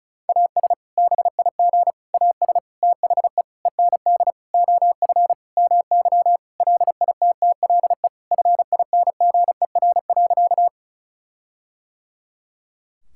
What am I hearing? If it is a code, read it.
Morse code, heard as a keyed tone, about 36 wpm, 700 Hz. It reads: AS BIG AS THE END OF MY LITTLE FINGER.
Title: Sentences from Top 1600 Words 35wpm